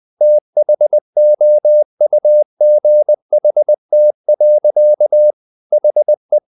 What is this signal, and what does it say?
Morse code, heard as a keyed tone, about 20 wpm, 600 Hz. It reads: THOUGHT. HE
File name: the old man and the sea part 2 20wpm